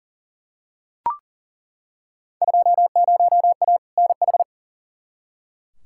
Morse code, heard as a keyed tone, about 37 wpm, 700 Hz. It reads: 20A NH
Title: Field Day Exchanges Encore - 40wpm